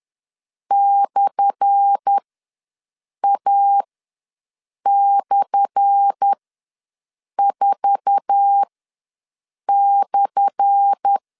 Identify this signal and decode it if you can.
Morse code, heard as a keyed tone, about 10 wpm, spaced as for 6 wpm, 800 Hz. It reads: /A/4/
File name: LEZIONI / 10 SIGNS / 10A SLASH ws 10wpm 4121